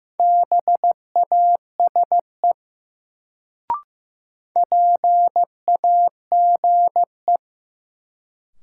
Morse code, heard as a keyed tone, about 15 wpm, 700 Hz. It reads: BASE PAGE
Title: Sentences from Top 1000 Words - Review Part 1 - New Word Drill - 15wpm